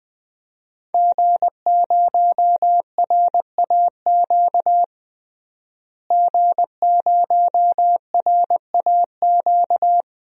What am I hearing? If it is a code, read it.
Morse code, heard as a keyed tone, about 20 wpm, 700 Hz. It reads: G0RAQ G0RAQ
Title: DXCC-100 Call Signs - 5 Characters - Repeated 2x - 20wpm